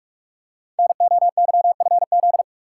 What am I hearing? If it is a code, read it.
Morse code, heard as a keyed tone, about 44 wpm, 700 Hz. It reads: NOYFZ
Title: Code Groups - 5 Letters - 45wpm